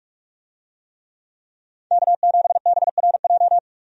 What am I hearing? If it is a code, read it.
Morse code, heard as a keyed tone, about 46 wpm, 700 Hz. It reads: K7BRJ
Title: USA Call Signs - 5 Characters - 45wpm